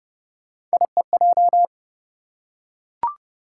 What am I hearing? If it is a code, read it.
Morse code, heard as a keyed tone, about 28 wpm, 700 Hz. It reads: IEJ